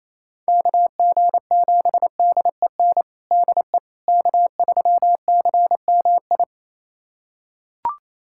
Morse code, heard as a keyed tone, about 28 wpm, 700 Hz. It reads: KG7DEN DE K3CMI
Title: USA Callsign DE Callsign 28wpm